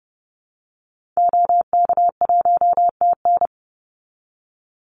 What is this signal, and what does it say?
Morse code, heard as a keyed tone, about 30 wpm, 700 Hz. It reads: OK1TN
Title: Call Signs Repeated 3x - 30wpm